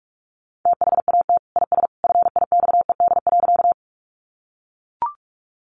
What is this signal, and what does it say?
Morse code, heard as a keyed tone, about 44 wpm, 700 Hz. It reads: THAT IS FIXED.
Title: Sentences from Top 1100 Words 45wpm